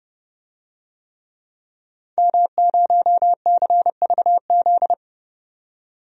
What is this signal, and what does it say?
Morse code, heard as a keyed tone, about 30 wpm, 700 Hz. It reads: M0CVZ
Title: DXCC-100 Call Signs - 5 Characters - 30wpm